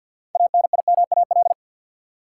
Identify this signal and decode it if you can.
Morse code, heard as a keyed tone, about 53 wpm, 700 Hz. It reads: ANIMAL
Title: Sentences from Top 1000 Words - Review Part 1 - New Word Drill - 50wpm